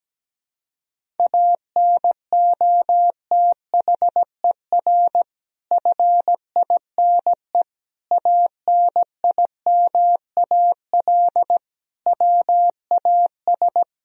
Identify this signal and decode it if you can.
Morse code, heard as a keyed tone, about 17 wpm, 700 Hz. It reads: ANOTHER FINE ANIMAL WAS
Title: Sentences from Top 1000 Words - Review Part 1 - 17wpm